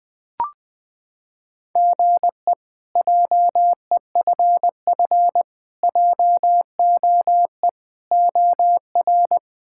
Morse code, heard as a keyed tone, about 20 wpm, 700 Hz. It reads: GE JEFF JOE OR
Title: K1USN SST QSO Practice - 20wpm